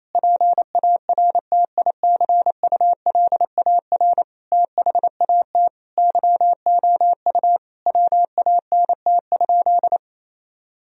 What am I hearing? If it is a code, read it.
Morse code, heard as a keyed tone, about 28 wpm, 700 Hz. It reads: PARTICULAR THAT YOU WANT?
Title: Sentences from Top 1000 Words - Review Part 4 - 28wpm